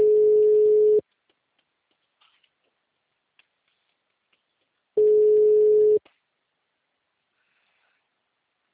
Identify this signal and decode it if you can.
Ringback tone (Europe, 425 Hz).